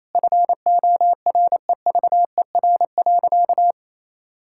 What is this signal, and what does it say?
Morse code, heard as a keyed tone, about 28 wpm, 700 Hz. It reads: FOREVER.
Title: Sentences from Top 1600 Words 28wpm